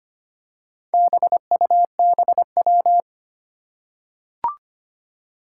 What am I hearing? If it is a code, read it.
Morse code, heard as a keyed tone, about 25 wpm, 700 Hz. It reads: BUBW
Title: Four Challenging Letter-Number Combinations 25wpm